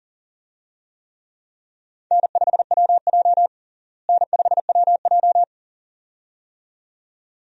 Morse code, heard as a keyed tone, about 37 wpm, 700 Hz. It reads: N5WJ N5WJ
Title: USA Call Signs - 4 Characters - Repeated 2x - 40wpm